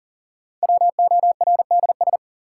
Morse code, heard as a keyed tone, about 40 wpm, 700 Hz. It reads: WORDS
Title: Sets of 3 Words - New Word Drill - 40wpm